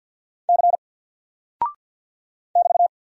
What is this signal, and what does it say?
Morse code, heard as a keyed tone, about 48 wpm, 700 Hz. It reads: C <BT>
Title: Single Character (Letter:Number:Punctuation:Prosign) Alt 50wpm